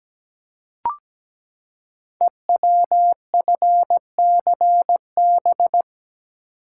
Morse code, heard as a keyed tone, about 17 wpm, 700 Hz. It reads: EWFCB
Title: Code Groups - 5 Letters - 17wpm